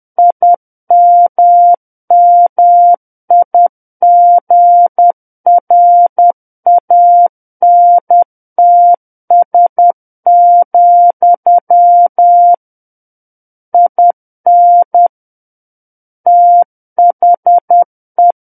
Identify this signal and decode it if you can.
Morse code, heard as a keyed tone, about 10 wpm, 700 Hz. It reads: IMMIGRANTS, IN THE
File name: Book One / 01-Chapter 1 - 10 WPM